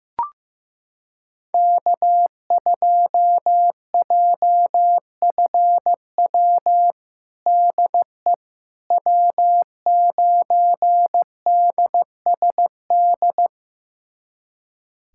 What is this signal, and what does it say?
Morse code, heard as a keyed tone, about 15 wpm, 700 Hz. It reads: K2JFW DE W9DSD
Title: USA Callsign DE Callsign - 5 Characters - 15wpm